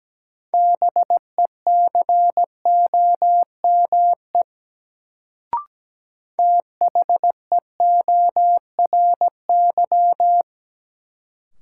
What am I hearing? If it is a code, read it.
Morse code, heard as a keyed tone, about 17 wpm, 700 Hz. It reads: BECOME THEORY
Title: Sentences from Top 600 Words - New Word Drill - 17wpm